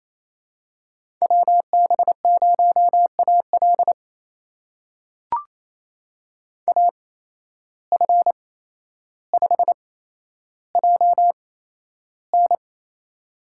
Morse code, heard as a keyed tone, about 28 wpm, spaced as for 12 wpm, 700 Hz. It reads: WB0AL AF5JN
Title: USA Call Signs - 5 Characters - 8x Character Spacing - 28wpm